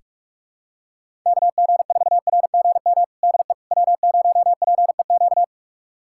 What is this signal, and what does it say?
Morse code, heard as a keyed tone, about 46 wpm, 700 Hz. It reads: KG4RGM DE W0PEQ
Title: USA Callsign DE Callsign 45wpm